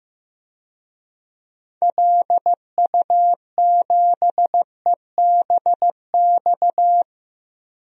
Morse code, heard as a keyed tone, about 15 wpm, 700 Hz. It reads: LU7EBX
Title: DXCC-100 Call Signs - 6 Characters - 15wpm